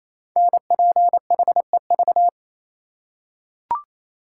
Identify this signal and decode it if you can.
Morse code, heard as a keyed tone, about 29 wpm, 700 Hz. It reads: NPHEV
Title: Code Groups - 5 Letters - 28wpm